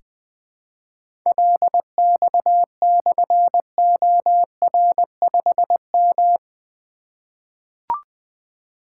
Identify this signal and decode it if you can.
Morse code, heard as a keyed tone, about 20 wpm, 700 Hz. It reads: LX/OR5M
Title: Difficult Call Signs 20wpm